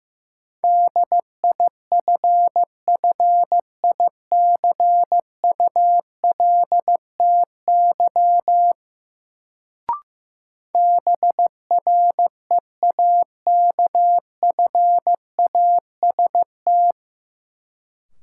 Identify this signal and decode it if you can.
Morse code, heard as a keyed tone, about 15 wpm, 700 Hz. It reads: DIFFICULTY BREAKFAST
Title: Sentences from Top 1500 Words - New Word Drill Alt - 15wpm